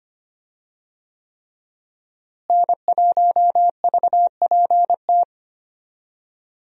Morse code, heard as a keyed tone, about 25 wpm, 700 Hz. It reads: N1VPT
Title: USA Call Signs - 5 Characters - Repeated 3x - 25wpm